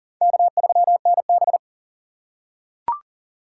Morse code, heard as a keyed tone, about 40 wpm, 700 Hz. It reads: K3NB
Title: USA Call Signs - 4 Characters - 4x Character Spacing - 40wpm